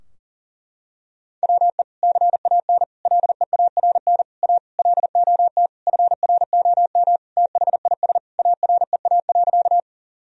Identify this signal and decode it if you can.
Morse code, heard as a keyed tone, about 40 wpm, 700 Hz. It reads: WE CAN LEARN A LOT FROM THIS AREA.
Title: Sentences from Top 1000 Words - Review Part 2 - 40wpm